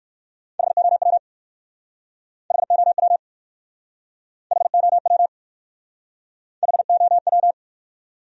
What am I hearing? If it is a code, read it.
Morse code, heard as a keyed tone, about 63 wpm, 700 Hz. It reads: HOW HOW HOW HOW